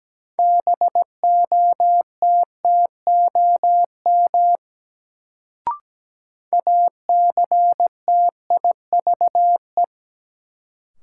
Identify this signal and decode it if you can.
Morse code, heard as a keyed tone, about 17 wpm, 700 Hz. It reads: BOTTOM ACTIVE
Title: Sentences from Top 900 Words - New Word Drill - 17wpm